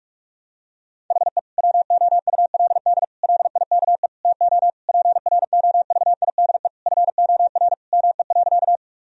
Morse code, heard as a keyed tone, about 46 wpm, 700 Hz. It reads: HE WOULD LIKE TO PROVIDE FOR ME.
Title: Sentences from Top 1000 Words - Review Part 3 - 45wpm